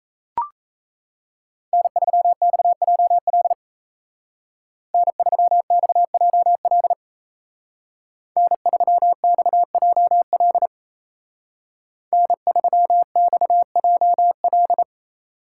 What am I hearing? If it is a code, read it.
Morse code, heard as a keyed tone, about 34 wpm, 700 Hz. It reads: N3XJL N3XJL N3XJL N3XJL
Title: Speed Racing - USA 1x3 Call Signs 28wpm